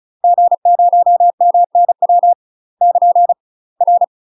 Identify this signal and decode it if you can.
Morse code, heard as a keyed tone, about 36 wpm, 700 Hz. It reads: G0MNW <KN> R
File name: UK MASTER QSOs TRAINING / 35WPM / UK MASTER QSOs 0013